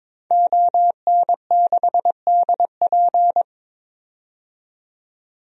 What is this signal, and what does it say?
Morse code, heard as a keyed tone, about 22 wpm, 700 Hz. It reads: ON6DP
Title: DXCC-100 Call Signs - 5 Characters - 22wpm